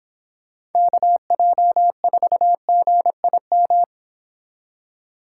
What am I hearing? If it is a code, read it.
Morse code, heard as a keyed tone, about 29 wpm, 700 Hz. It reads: KJ4GIM